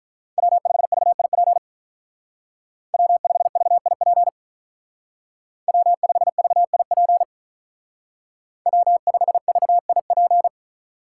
Morse code, heard as a keyed tone, about 45 wpm, 700 Hz. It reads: W5VIP W5VIP W5VIP W5VIP